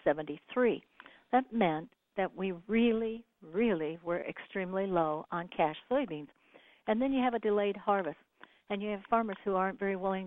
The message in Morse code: none

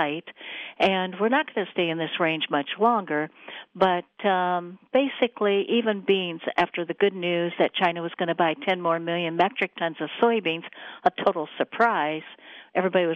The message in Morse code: none

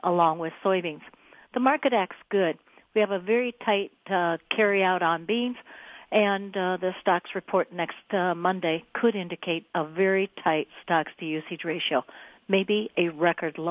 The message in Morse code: none

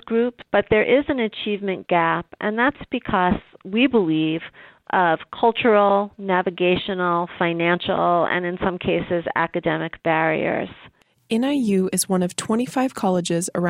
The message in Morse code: none

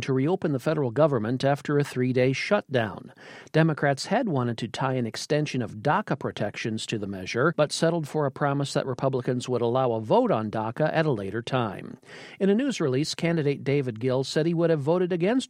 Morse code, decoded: none

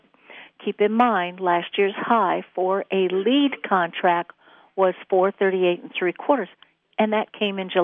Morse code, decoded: none

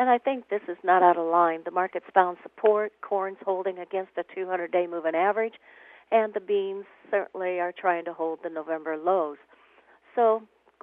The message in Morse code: none

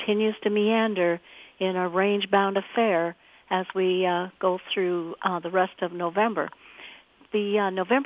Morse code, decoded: none

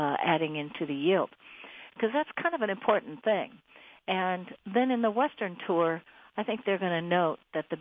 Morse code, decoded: none

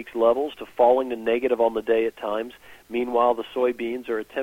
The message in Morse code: none

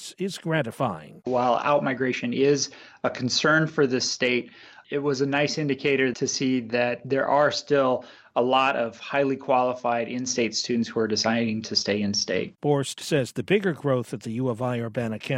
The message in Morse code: none

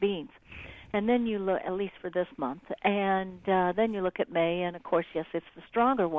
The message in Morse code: none

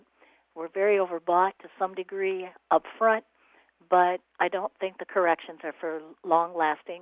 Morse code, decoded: none

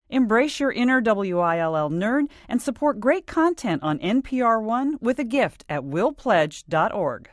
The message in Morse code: none

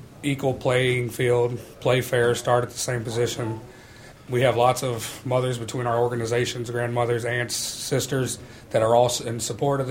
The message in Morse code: none

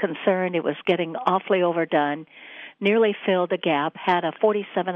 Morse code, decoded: none

